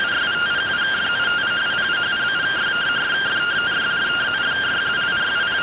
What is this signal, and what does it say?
Two-tone teleprinter signal, about 170 Hz shift.